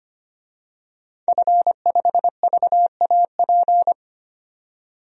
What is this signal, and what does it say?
Morse code, heard as a keyed tone, about 25 wpm, 700 Hz. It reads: F5VAP
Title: DXCC-100 Call Signs - 5 Characters - 25wpm